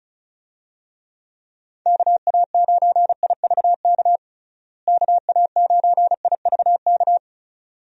Morse code, heard as a keyed tone, about 36 wpm, 700 Hz. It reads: KA9IVK KA9IVK
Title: USA Call Signs - 6 Characters - Repeated 2x - 35wpm